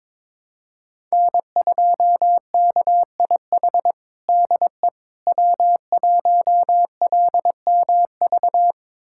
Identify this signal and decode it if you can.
Morse code, heard as a keyed tone, about 22 wpm, 700 Hz. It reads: N2KIH DE W1LMV
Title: USA Callsign DE Callsign - 5 Characters - 22wpm